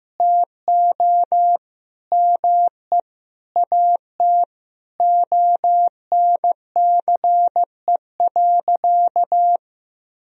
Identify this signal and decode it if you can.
Morse code, heard as a keyed tone, about 15 wpm, 700 Hz. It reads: TO ME AT ONCE.
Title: Sentences from Top 500 Words and Abbreviations 15wpm